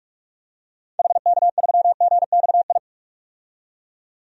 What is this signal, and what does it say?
Morse code, heard as a keyed tone, about 42 wpm, 700 Hz. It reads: HK3GXI
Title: DXCC-100 Call Signs - 6 Characters - 45wpm